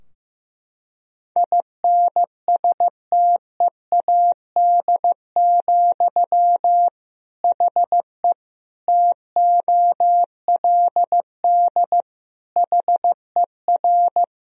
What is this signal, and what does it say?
Morse code, heard as a keyed tone, about 15 wpm, 700 Hz. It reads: INSTEAD, HE TOLD HER
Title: Sentences from Top 1000 Words - Review Part 4 - 15wpm